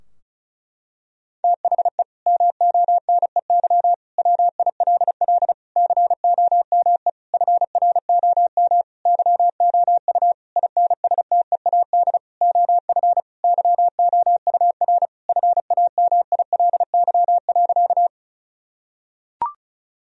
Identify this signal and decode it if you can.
Morse code, heard as a keyed tone, about 36 wpm, 700 Hz. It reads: THE MONEY WILL COME FROM YOU INSTEAD OF YOUR FAMILY.
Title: Sentences from Top 1000 Words - Review Part 4 - 35wpm